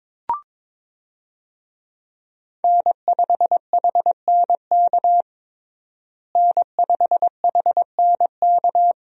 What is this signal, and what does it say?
Morse code, heard as a keyed tone, about 22 wpm, 700 Hz. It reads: N5HNK N5HNK
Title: USA Call Signs - 5 Characters - Repeated 2x - 22wpm